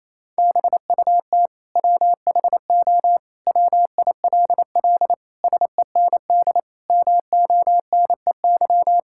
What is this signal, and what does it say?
Morse code, heard as a keyed tone, about 28 wpm, 700 Hz. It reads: BUT WHO WILL SEND MONEY
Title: Sentences from Top 1600 Words 28wpm